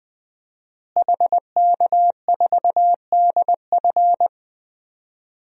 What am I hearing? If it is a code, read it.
Morse code, heard as a keyed tone, about 20 wpm, 700 Hz. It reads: HK4DF
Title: DXCC-100 Call Signs - 5 Characters - Repeated 2x - 20wpm